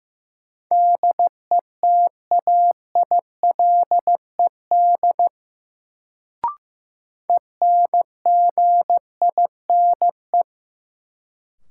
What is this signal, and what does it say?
Morse code, heard as a keyed tone, about 15 wpm, 700 Hz. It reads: DETAILED ENGINE
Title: Sentences from Top 1500 Words - New Word Drill - 15wpm